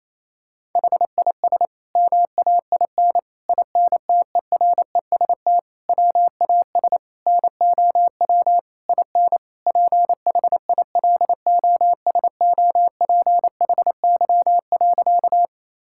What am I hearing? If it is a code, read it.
Morse code, heard as a keyed tone, about 28 wpm, 700 Hz. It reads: HIS MAIN INTEREST WAS NOW IN PHILOSOPHY.
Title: Sentences from Top 1000 Words - Review Part 6 - 28wpm